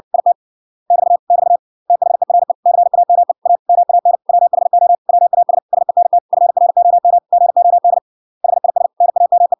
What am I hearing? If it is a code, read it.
Morse code, heard as a keyed tone, about 63 wpm, 700 Hz. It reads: IT <BT> <BT> THERE CAME A MAN WHO WAS SENT FROM GOD HIS NAME